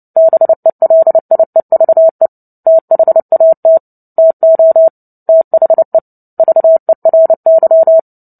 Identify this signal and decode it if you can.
Morse code, heard as a keyed tone, about 30 wpm, 650 Hz. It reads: BELIEVE THAT TO THE VERY
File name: Book Two / 03-Chapter 3 - 29 WPM